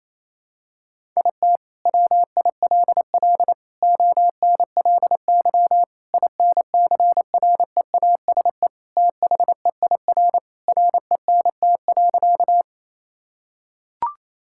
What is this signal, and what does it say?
Morse code, heard as a keyed tone, about 28 wpm, 700 Hz. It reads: IT WILL ONLY INCREASE THEIR RENT.
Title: Sentences from Top 1000 Words - Review Part 5 - 28wpm